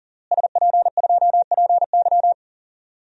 Morse code, heard as a keyed tone, about 40 wpm, 700 Hz. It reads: SP2PY
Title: DXCC-100 Call Signs - 5 Characters - Repeated 2x - 40wpm